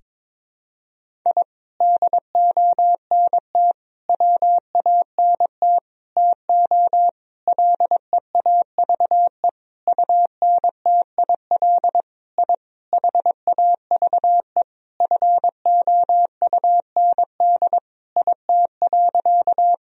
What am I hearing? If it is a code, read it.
Morse code, heard as a keyed tone, about 22 wpm, 700 Hz. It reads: I DONT WANT TO LEAVE UNTIL I HAVE FOUND IT.
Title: Sentences from Top 1000 Words - Review Part 1 - 22wpm